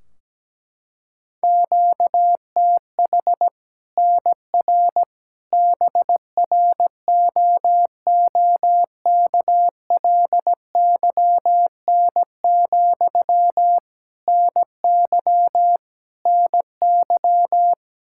Morse code, heard as a keyed tone, about 17 wpm, 700 Hz. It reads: QTH NR BROOKLYN, NY NY
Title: Practice QSO - Exchange 1 - Beginning - 17wpm